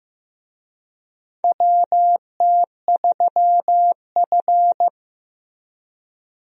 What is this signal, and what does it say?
Morse code, heard as a keyed tone, about 15 wpm, 700 Hz. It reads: WT3F
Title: USA Call Signs - 4 Characters - Repeated 3x - 15wpm